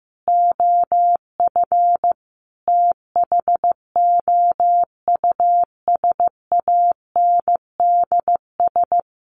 Morse code, heard as a keyed tone, about 15 wpm, 700 Hz. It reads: OF THOUSANDS
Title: Sets of 4 Words 15wpm